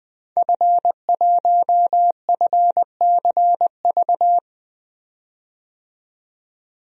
Morse code, heard as a keyed tone, about 20 wpm, 700 Hz. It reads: F1FCV
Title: DXCC-100 Call Signs - 5 Characters - 20wpm